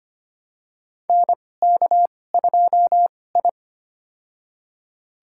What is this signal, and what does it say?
Morse code, heard as a keyed tone, about 25 wpm, spaced as for 18 wpm, 700 Hz. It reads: NK2I
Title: USA Call Signs - 2x Character Spacing - 25wpm